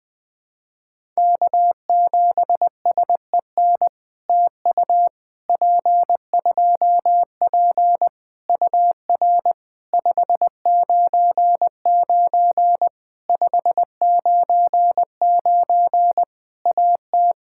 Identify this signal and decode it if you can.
Morse code, heard as a keyed tone, about 20 wpm, 700 Hz. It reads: K7SEN TU P2P UR 599 599 AT
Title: POTA QSO Practice - 20wpm